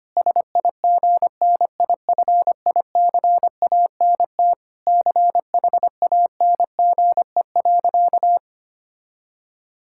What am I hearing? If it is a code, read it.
Morse code, heard as a keyed tone, about 25 wpm, 700 Hz. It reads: SIGNIFICANT CHANGE.
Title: Sentences from Top 1000 Words - Review Part 6 - 25wpm